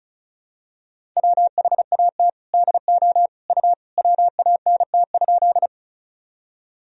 Morse code, heard as a keyed tone, about 36 wpm, 700 Hz. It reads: WHAT DO U WANT?
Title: Sentences from Top 200 Words and Abbreviations 35wpm